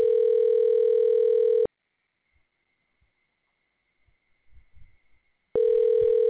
Ringback tone (North America), 440 + 480 Hz.